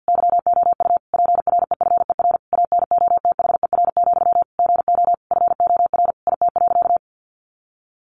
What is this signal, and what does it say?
Morse code, heard as a keyed tone, about 50 wpm, 700 Hz. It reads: YOU PREFER ANOTHER, GO FOR IT.
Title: Sentences from Top 1100 Words 50wpm